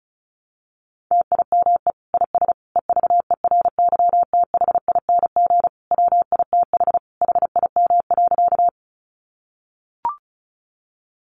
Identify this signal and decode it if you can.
Morse code, heard as a keyed tone, about 34 wpm, 700 Hz. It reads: TIME IS EVERYTHING WITH HIM.
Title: Sentences from Top 700 Words 35wpm